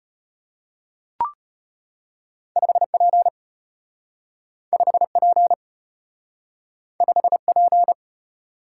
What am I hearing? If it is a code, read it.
Morse code, heard as a keyed tone, about 33 wpm, 700 Hz. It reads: 5P 5P 5P